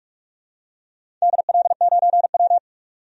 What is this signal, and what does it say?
Morse code, heard as a keyed tone, about 45 wpm, 700 Hz. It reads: DL9W
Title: DXCC-100 Call Signs - 4 Characters - 45wpm